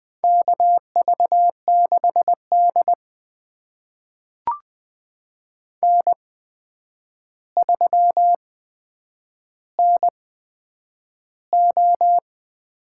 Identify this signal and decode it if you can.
Morse code, heard as a keyed tone, about 21 wpm, spaced as for 9 wpm, 700 Hz. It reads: KV6D N3NO
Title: USA Call Signs - 4 Characters - 8x Character Spacing - 20wpm